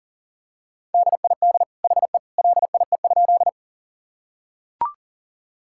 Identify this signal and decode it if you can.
Morse code, heard as a keyed tone, about 40 wpm, 700 Hz. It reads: DID HE LIE?
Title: Sentences from Top 1600 Words 40wpm